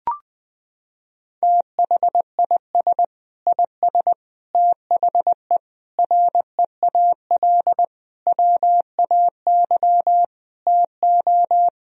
Morse code, heard as a keyed tone, about 21 wpm, 700 Hz. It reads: THIS IS THE REAL WAY TO
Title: Sentences from Top 1000 Words - Review Part 2 - 20wpm